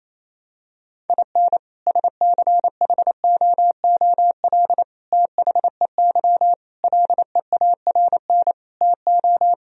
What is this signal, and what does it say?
Morse code, heard as a keyed tone, about 28 wpm, 700 Hz. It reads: IN SCHOOL THEY LEARN TO
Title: Sentences from Top 1000 Words - Review Part 4 - 28wpm